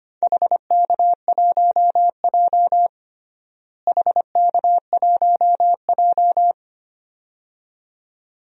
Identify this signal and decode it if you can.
Morse code, heard as a keyed tone, about 25 wpm, 700 Hz. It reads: HK1J HK1J
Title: DXCC-100 Call Signs - 4 Characters - Repeated 2x - 25wpm